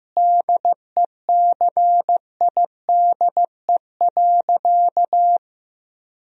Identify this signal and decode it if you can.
Morse code, heard as a keyed tone, about 15 wpm, 700 Hz. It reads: DECIDE.
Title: Sentences from Top 1000 Words - Review Part 3 - 15wpm